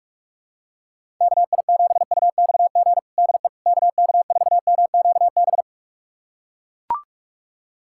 Morse code, heard as a keyed tone, about 41 wpm, 700 Hz. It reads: KI7UXG DE KK4MQB
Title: USA Callsign DE Callsign - 6 Characters - 45wpm